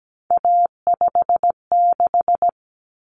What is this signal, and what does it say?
Morse code, heard as a keyed tone, about 17 wpm, 700 Hz. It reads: A56
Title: Common Cut Numbers - 3-Digit 17wpm